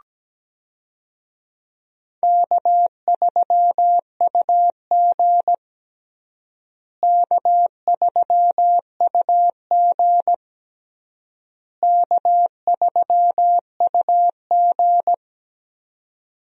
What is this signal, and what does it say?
Morse code, heard as a keyed tone, about 17 wpm, 700 Hz. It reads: K3UG K3UG K3UG
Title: USA Call Signs - 4 Characters - Repeated 3x - 17wpm